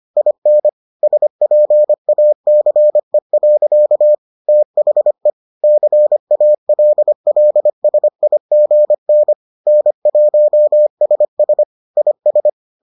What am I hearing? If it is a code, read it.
Morse code, heard as a keyed tone, about 25 wpm, 600 Hz. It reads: IN SPACE. THE CALLSIGN N1SS IS